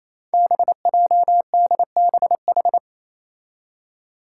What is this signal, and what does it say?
Morse code, heard as a keyed tone, about 28 wpm, 700 Hz. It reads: BJDBH
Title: Five Challenging Letter-Number Combinations 28wpm